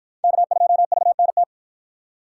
Morse code, heard as a keyed tone, about 51 wpm, 700 Hz. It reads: K2VNT